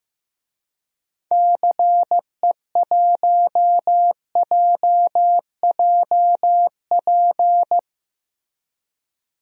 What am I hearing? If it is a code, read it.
Morse code, heard as a keyed tone, about 15 wpm, 700 Hz. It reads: CE1JJP
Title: DXCC-100 Call Signs - 6 Characters - Repeated 2x - 15wpm